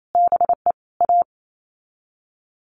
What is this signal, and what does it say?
Morse code, heard as a keyed tone, about 28 wpm, 700 Hz. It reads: BE A
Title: Sets of 5 Words Encore2 28wpm